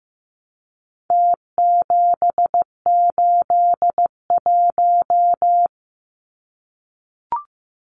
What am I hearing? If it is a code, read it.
Morse code, heard as a keyed tone, about 15 wpm, 700 Hz. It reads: T781
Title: Numbers - Contest Style - 4-Digit 15wpm